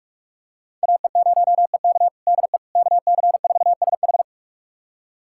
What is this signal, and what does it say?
Morse code, heard as a keyed tone, about 45 wpm, 700 Hz. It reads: AE0EK DE KC4SH
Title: USA Callsign DE Callsign - 5 Characters - 45wpm